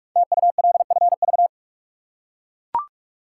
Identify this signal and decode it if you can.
Morse code, heard as a keyed tone, about 44 wpm, 700 Hz. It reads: TULFV
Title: Code Groups - 5 Letters - 45wpm